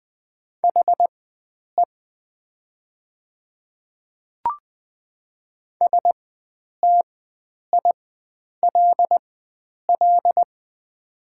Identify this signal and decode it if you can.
Morse code, heard as a keyed tone, about 20 wpm, spaced as for 9 wpm, 700 Hz. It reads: HE STILL
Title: 4x Character Spacing - Top 300 Words - 20wpm